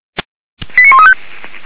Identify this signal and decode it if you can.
Five-tone selective call (ZVEI), heard as quick runs of five tones, tones of about 70 ms. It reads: 89135
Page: /464/SDR/ZVEI/